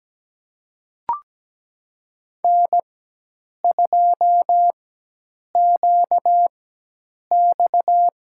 Morse code, heard as a keyed tone, about 17 wpm, spaced as for 8 wpm, 700 Hz. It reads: N2QX
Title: USA Call Signs - 5 Characters - 4x Character Spacing - 17wpm